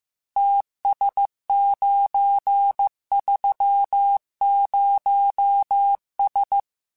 Morse code, heard as a keyed tone, about 15 wpm, 800 Hz. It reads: TS930S
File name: QSOs / QSOs-mp3s / kc7sac@13